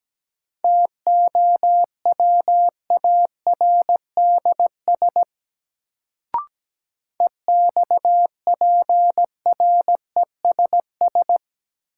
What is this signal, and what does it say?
Morse code, heard as a keyed tone, about 17 wpm, 700 Hz. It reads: TOWARDS EXPRESS
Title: Sentences from Top 1100 Words - New Word Drill Alt - 17wpm